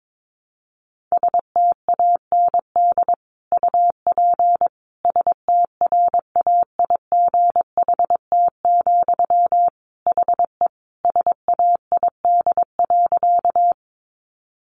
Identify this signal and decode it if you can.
Morse code, heard as a keyed tone, about 22 wpm, 700 Hz. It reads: STAND UP STRAIGHT, HE SAID.
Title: Sentences from Top 1000 Words 22wpm